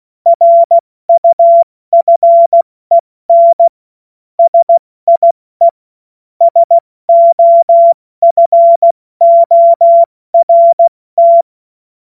Morse code, heard as a keyed tone, about 16 wpm, 680 Hz. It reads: RUFEN SIE SOFORT
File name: Morsekurs / Witze / Witz 26 März 2024 70 F80